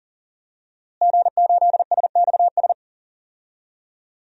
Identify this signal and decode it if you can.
Morse code, heard as a keyed tone, about 40 wpm, 700 Hz. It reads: G8SXS
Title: DXCC-100 Call Signs - 5 Characters - Repeated 3x - 40wpm